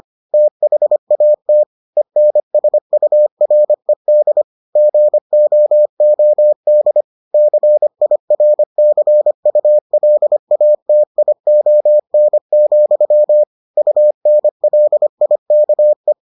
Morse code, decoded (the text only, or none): THAT ENSURED GOOD CIRCULATION, UNLIKE